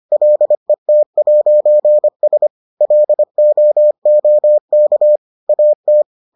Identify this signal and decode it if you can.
Morse code, heard as a keyed tone, about 25 wpm, 600 Hz. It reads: LET'S LOOK AT